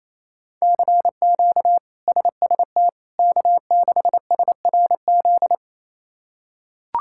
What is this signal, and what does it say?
Morse code, heard as a keyed tone, about 27 wpm, 700 Hz. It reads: CQ SST K6SRZ
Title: K1USN SST QSO Practice - 28wpm